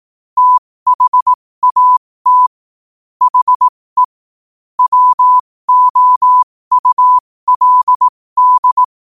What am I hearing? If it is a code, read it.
Morse code, heard as a keyed tone, about 18 wpm, spaced as for 15 wpm, 1000 Hz. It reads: THAT HE WOULD
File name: qotdfiles / 15wpm